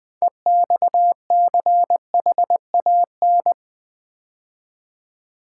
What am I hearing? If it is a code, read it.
Morse code, heard as a keyed tone, about 20 wpm, 700 Hz. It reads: EXCHAN
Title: Build A Word - 2 Letters At A Time - 8-Letter Words - 20wpm